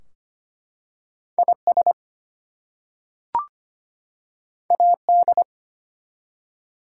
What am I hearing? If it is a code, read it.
Morse code, heard as a keyed tone, about 25 wpm, 700 Hz. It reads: IS AD
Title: Two Letter Words - 25wpm